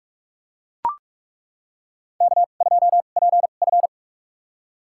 Morse code, heard as a keyed tone, about 41 wpm, spaced as for 31 wpm, 700 Hz. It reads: K2PF
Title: USA Call Signs - 4 Characters - 2x Character Spacing - 45wpm